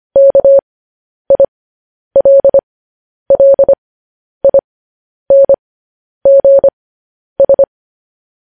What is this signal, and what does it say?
Morse code, heard as a keyed tone, about 25 wpm, spaced as for 10 wpm, 550 Hz. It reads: KILLINGS
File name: podcast / News headlines-1x-25x10